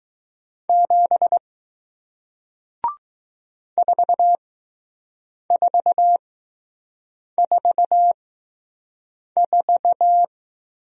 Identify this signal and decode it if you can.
Morse code, heard as a keyed tone, about 20 wpm, spaced as for 6 wpm, 700 Hz. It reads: 7 4444